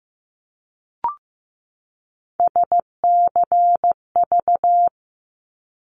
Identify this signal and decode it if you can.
Morse code, heard as a keyed tone, about 15 wpm, 700 Hz. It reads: SCV
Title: ARRL Sections 15wpm